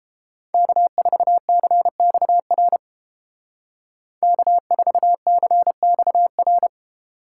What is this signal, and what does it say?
Morse code, heard as a keyed tone, about 31 wpm, 700 Hz. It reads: K4CXR K4CXR